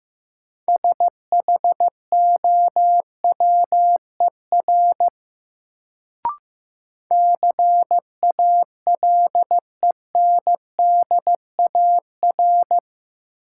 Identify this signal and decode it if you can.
Morse code, heard as a keyed tone, about 15 wpm, 700 Hz. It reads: SHOWER CALENDAR
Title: Sentences from Top 1600 Words - New Word Drill - 15wpm